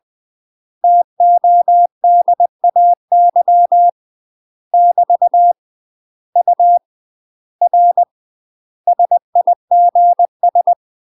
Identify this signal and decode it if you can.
Morse code, heard as a keyed tone, about 20 wpm, 700 Hz. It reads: TODAY <BT> U R SIGS
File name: UK MASTER QSOs TRAINING / 20WPM / UK MASTER QSOs 0059